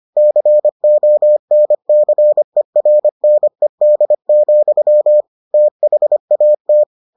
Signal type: Morse code

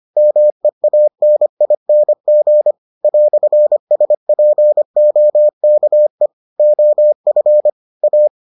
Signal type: Morse code

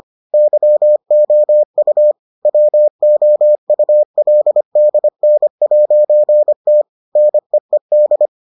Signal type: Morse code

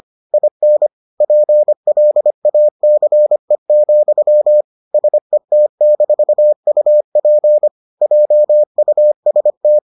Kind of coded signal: Morse code

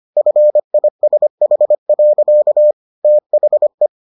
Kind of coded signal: Morse code

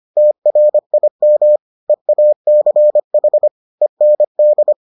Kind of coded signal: Morse code

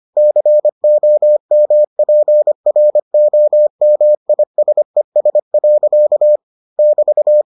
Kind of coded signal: Morse code

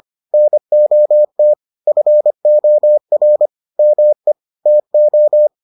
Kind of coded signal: Morse code

